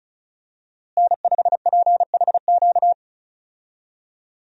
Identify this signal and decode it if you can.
Morse code, heard as a keyed tone, about 36 wpm, 700 Hz. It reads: N5PHQ
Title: USA Call Signs - 5 Characters - Repeated 2x - 35wpm